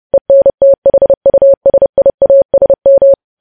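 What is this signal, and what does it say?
Morse code, heard as a keyed tone, about 30 wpm, 550 Hz. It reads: ENTHUSIASM
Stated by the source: News headlines sent in Morse at 30 wpm, preceded by VVV.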